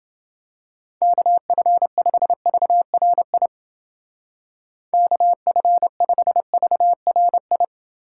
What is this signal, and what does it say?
Morse code, heard as a keyed tone, about 29 wpm, 700 Hz. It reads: KF5VRI KF5VRI